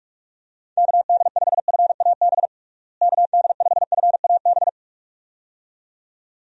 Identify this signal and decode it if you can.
Morse code, heard as a keyed tone, about 43 wpm, 700 Hz. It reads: KD5FAB KD5FAB
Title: USA Call Signs - 6 Characters - Repeated 2x - 45wpm